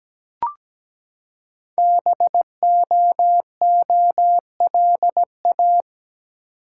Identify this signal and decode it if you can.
Morse code, heard as a keyed tone, about 17 wpm, 700 Hz. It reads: BOOLA
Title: Code Groups - 5 Letters - 17wpm